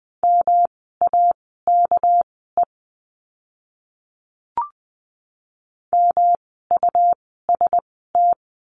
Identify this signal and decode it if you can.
Morse code, heard as a keyed tone, about 20 wpm, spaced as for 14 wpm, 700 Hz. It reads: MAKE MUST
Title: 237 Lesson 38f - Letter q - Words 20wpm